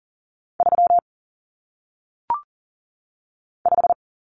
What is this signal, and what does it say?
Morse code, heard as a keyed tone, about 40 wpm, 700 Hz. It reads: 3 5
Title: Common Cut Numbers - 1-Digit 40wpm